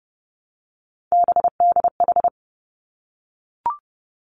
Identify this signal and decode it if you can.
Morse code, heard as a keyed tone, about 30 wpm, 700 Hz. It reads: BDH
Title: Three Challenging Letter-Number Combinations 30wpm